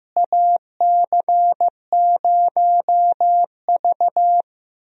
Morse code, heard as a keyed tone, about 15 wpm, 700 Hz. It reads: AC0V
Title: USA Call Signs - 4 Characters - Repeated 3x - 15wpm